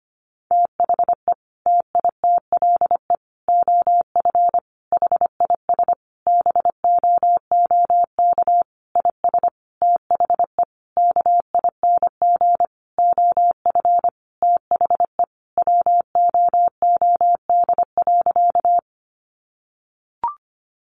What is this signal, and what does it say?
Morse code, heard as a keyed tone, about 25 wpm, 700 Hz. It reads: THE TITLE OF HIS BOOK IS THE KING OF THE WOOD.
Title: Sentences from Top 1200 Words 25wpm